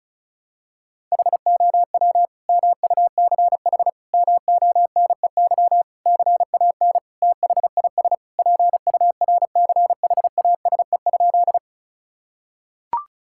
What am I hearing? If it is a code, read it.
Morse code, heard as a keyed tone, about 34 wpm, 700 Hz. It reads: HOW MUCH MONEY CAN THIS PURCHASE?
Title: Sentences from Top 1000 Words - Review Part 5 - 35wpm